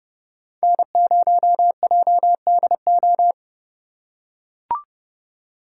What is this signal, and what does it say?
Morse code, heard as a keyed tone, about 29 wpm, 700 Hz. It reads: N0JDO